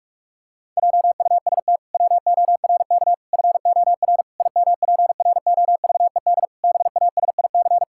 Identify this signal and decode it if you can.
Morse code, heard as a keyed tone, about 41 wpm, 700 Hz. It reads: JUST WORK FOR IMPROVED BASIC
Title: Pangram Challenge v2 - 45wpm